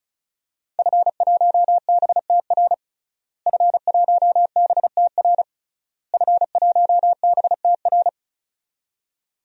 Morse code, heard as a keyed tone, about 36 wpm, 700 Hz. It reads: F1BTR F1BTR F1BTR
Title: DXCC-100 Call Signs - 5 Characters - Repeated 3x - 35wpm